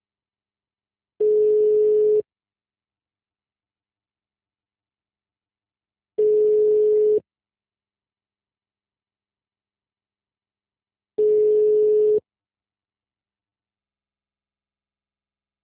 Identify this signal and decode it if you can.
Ringback tone (Europe, 425 Hz).